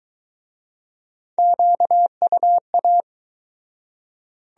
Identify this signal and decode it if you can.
Morse code, heard as a keyed tone, about 23 wpm, 700 Hz. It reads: QUA